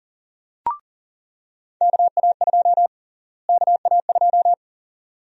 Morse code, heard as a keyed tone, about 40 wpm, 700 Hz. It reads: KA2 KA2
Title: USA Call Sign Prefixes - Repeated 2x - 40wpm